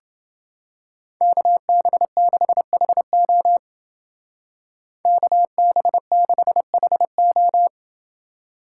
Morse code, heard as a keyed tone, about 29 wpm, 700 Hz. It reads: KB6HO KB6HO